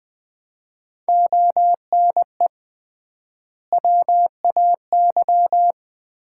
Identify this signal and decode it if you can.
Morse code, heard as a keyed tone, about 20 wpm, 700 Hz. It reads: ONE WAY
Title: 3x Word-Spacing - Sets of 5 Words - 20wpm